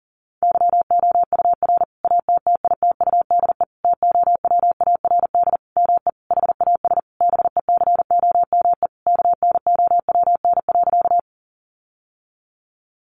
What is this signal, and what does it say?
Morse code, heard as a keyed tone, about 40 wpm, 700 Hz. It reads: YOUR ATTITUDE TOWARD ME HAS BECOME KNOWN.
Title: Sentences from Top 1300 Words 40wpm